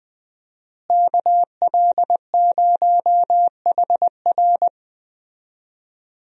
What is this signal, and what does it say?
Morse code, heard as a keyed tone, about 20 wpm, 700 Hz. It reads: KL0HR
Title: USA Call Signs - 5 Characters - Repeated 3x - 20wpm